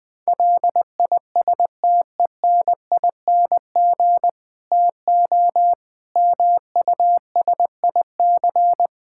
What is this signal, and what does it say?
Morse code, heard as a keyed tone, about 20 wpm, 700 Hz. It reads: LISTENING TO MUSIC
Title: Difficult Sentences with Hints - 20wpm